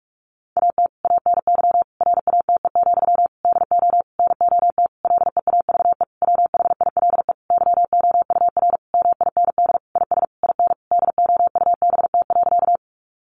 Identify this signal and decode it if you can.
Morse code, heard as a keyed tone, about 44 wpm, 700 Hz. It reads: AT ANY RATE, DO NOT LEAVE WHILE YOUR MIND IS IN DOUBT.
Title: Sentences from Top 1500 Words 45wpm